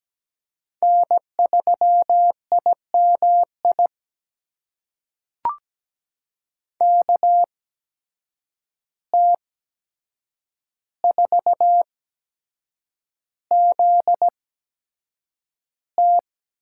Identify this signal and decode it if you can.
Morse code, heard as a keyed tone, about 17 wpm, spaced as for 7 wpm, 700 Hz. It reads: N3IMI KT4ZT
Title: USA Call Signs - 5 Characters - 8x Character Spacing - 17wpm